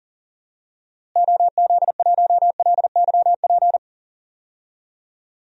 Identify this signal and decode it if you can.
Morse code, heard as a keyed tone, about 40 wpm, 700 Hz. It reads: OZ1LYP
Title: DXCC-100 Call Signs - 6 Characters - Repeated 3x - 40wpm